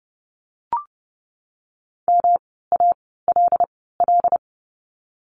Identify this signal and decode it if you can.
Morse code, heard as a keyed tone, about 30 wpm, spaced as for 17 wpm, 700 Hz. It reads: MALL
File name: course / 30:10wpm / 108 Lesson 18f - Letter m - Words 30wpm